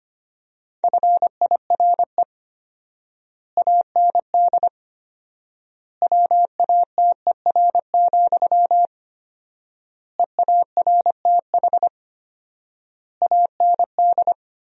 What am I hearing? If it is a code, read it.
Morse code, heard as a keyed tone, about 25 wpm, 700 Hz. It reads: FIRE AND WATER, EARTH AND
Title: 4x Word-Spacing - Sentences from Top 400 Words - 25wpm